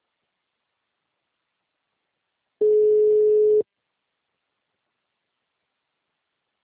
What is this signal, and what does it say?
Ringback tone (Europe, 425 Hz).
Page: Itu